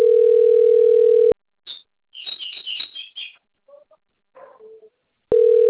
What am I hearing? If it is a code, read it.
Ringback tone (North America), 440 + 480 Hz.
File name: rg-1600-unknown- - - .78486